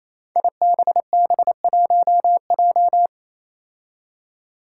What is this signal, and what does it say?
Morse code, heard as a keyed tone, about 29 wpm, 700 Hz. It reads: IBB1J
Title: Five Challenging Letter-Number Combinations 28wpm